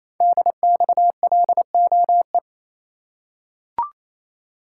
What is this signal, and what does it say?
Morse code, heard as a keyed tone, about 27 wpm, 700 Hz. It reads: DXLOE
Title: Code Groups - 5 Letters - 28wpm